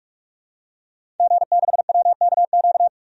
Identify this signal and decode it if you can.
Morse code, heard as a keyed tone, about 46 wpm, 700 Hz. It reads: G6WKQ
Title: DXCC-100 Call Signs - 5 Characters - 45wpm